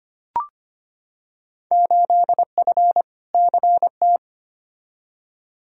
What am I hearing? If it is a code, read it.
Morse code, heard as a keyed tone, about 25 wpm, 700 Hz. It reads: 8F CT
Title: Field Day Exchanges Encore - 25wpm